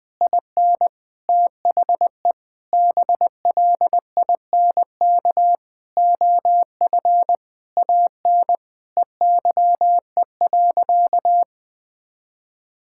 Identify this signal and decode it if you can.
Morse code, heard as a keyed tone, about 20 wpm, 700 Hz. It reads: IN THE BLINK OF AN EYE.
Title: Sentences with 6-Word Phrases - 20wpm